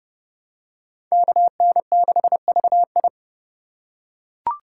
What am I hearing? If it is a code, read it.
Morse code, heard as a keyed tone, about 29 wpm, 700 Hz. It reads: KN6VI